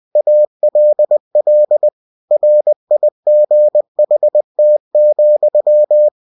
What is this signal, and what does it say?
Morse code, heard as a keyed tone, about 20 wpm, 600 Hz. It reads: ALL RIGHT,
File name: winnie-20wpm